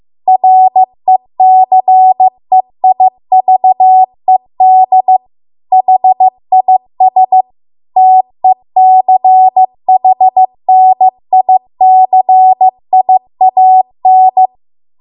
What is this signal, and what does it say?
Morse code, heard as a keyed tone, about 15 wpm, 750 Hz. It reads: RECEIVED HIS TECHNICIAN